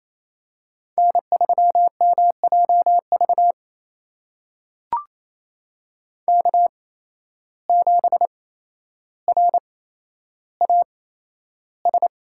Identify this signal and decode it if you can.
Morse code, heard as a keyed tone, about 28 wpm, spaced as for 12 wpm, 700 Hz. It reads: N3MJV K7RAS
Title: USA Call Signs - 5 Characters - 8x Character Spacing - 28wpm